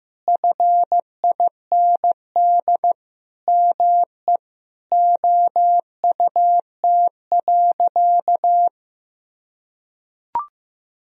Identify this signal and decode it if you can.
Morse code, heard as a keyed tone, about 14 wpm, 700 Hz. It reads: FIND ME OUT.
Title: Sentences from Top 500 Words and Abbreviations 15wpm